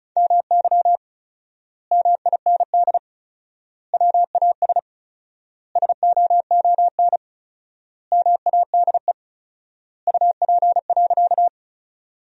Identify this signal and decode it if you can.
Morse code, heard as a keyed tone, about 34 wpm, 700 Hz. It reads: MY MIND WAS SOON MADE UP.
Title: 4x Word-Spacing - Sentences from Top 500 Words - 35wpm